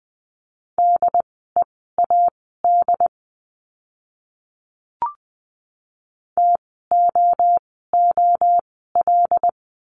Morse code, heard as a keyed tone, about 20 wpm, spaced as for 14 wpm, 700 Hz. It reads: DEAD TOOL
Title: 075 Lesson 13f - Letter L - Words 20wpm